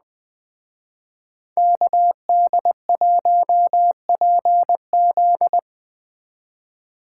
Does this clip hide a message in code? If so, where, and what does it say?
Yes: Morse code from 1.6 to 5.6 s, heard as a keyed tone, about 20 wpm, 700 Hz. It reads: KD1PZ